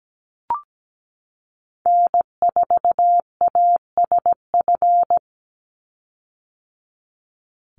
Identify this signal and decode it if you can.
Morse code, heard as a keyed tone, about 17 wpm, 700 Hz. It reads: N4ASF
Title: Call Signs - 17wpm